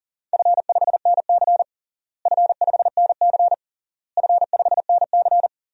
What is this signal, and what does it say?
Morse code, heard as a keyed tone, about 40 wpm, 700 Hz. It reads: F5NC F5NC F5NC
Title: DXCC-100 Call Signs - 4 Characters - Repeated 3x - 40wpm